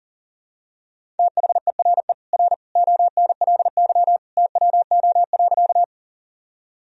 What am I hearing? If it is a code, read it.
Morse code, heard as a keyed tone, about 40 wpm, 700 Hz. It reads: THERE R ONLY TWO.
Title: Sentences from Top 200 Words and Abbreviations 40wpm